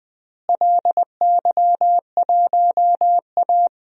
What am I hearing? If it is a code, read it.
Morse code, heard as a keyed tone, about 20 wpm, 700 Hz. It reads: LY1A
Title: DXCC-100 Call Signs - 4 Characters - Repeated 2x - 20wpm